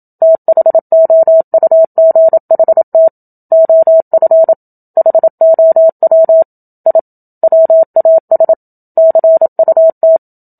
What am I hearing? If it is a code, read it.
Morse code, heard as a keyed tone, about 27 wpm, 650 Hz. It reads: THOUGHT OF HOW I WAS CUT
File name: Book Two / 01-Chapter 1 - 27 WPM